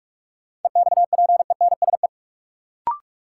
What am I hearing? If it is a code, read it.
Morse code, heard as a keyed tone, about 46 wpm, 700 Hz. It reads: EXPENSE
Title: Sets of 4 Words - New Word Drill - 45wpm